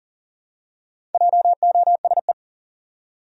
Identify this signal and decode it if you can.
Morse code, heard as a keyed tone, about 37 wpm, 700 Hz. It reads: JOSE
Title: USA Common Names - Max 4 Letters - 40wpm